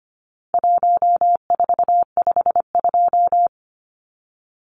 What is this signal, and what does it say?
Morse code, heard as a keyed tone, about 25 wpm, 700 Hz. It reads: 1452
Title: Numbers - Contest Style - 4-Digit 25wpm